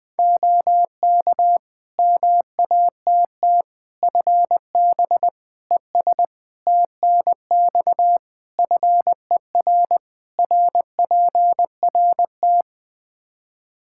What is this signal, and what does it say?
Morse code, heard as a keyed tone, about 20 wpm, 700 Hz. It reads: OK MATT FB ES TNX FER RPRT
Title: Practice QSO - Exchange 2 - Info - 20wpm